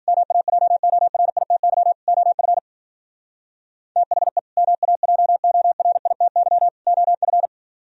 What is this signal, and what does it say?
Morse code, heard as a keyed tone, about 51 wpm, 700 Hz. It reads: MAJORITY OF THE MAJORITY OF